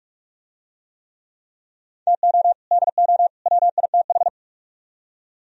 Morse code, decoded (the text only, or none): TO DO WITH